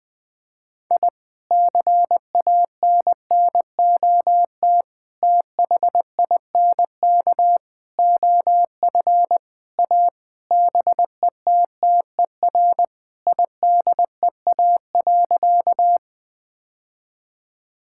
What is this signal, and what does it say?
Morse code, heard as a keyed tone, about 20 wpm, 700 Hz. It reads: I CANNOT THINK OF A BETTER IDEA.
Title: Sentences with 5-Word Phrases - 20wpm